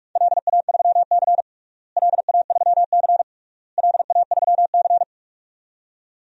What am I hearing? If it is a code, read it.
Morse code, heard as a keyed tone, about 46 wpm, 700 Hz. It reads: LA3C LA3C LA3C
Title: DXCC-100 Call Signs - 4 Characters - Repeated 3x - 45wpm